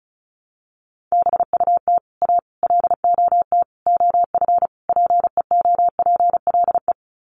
Morse code, heard as a keyed tone, about 34 wpm, 700 Hz. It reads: BUT A LOT OF PEOPLE
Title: Sets of 5 Words Encore2 35wpm